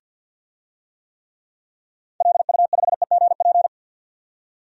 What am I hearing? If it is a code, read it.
Morse code, heard as a keyed tone, about 52 wpm, 700 Hz. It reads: LU5EGP
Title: DXCC-100 Call Signs - 6 Characters - 50wpm